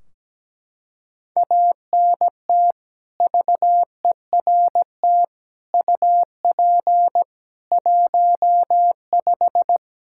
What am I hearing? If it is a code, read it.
Morse code, heard as a keyed tone, about 17 wpm, 700 Hz. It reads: ANT VERT UP 15
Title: K1USN SST QSO Practice 17wpm